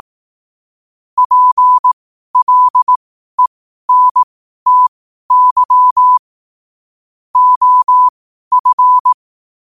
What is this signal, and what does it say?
Morse code, heard as a keyed tone, about 18 wpm, spaced as for 12 wpm, 1000 Hz. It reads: PLENTY OF
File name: qotdfiles / 12wpm